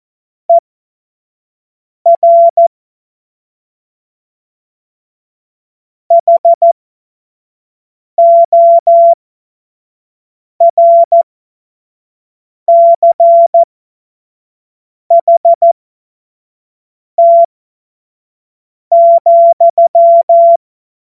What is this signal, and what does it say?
Morse code, heard as a keyed tone, about 14 wpm, spaced as for 5 wpm, 680 Hz. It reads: ER HORCHT,
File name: Morsekurs / Witze / Witz 02 April 2024 25 F70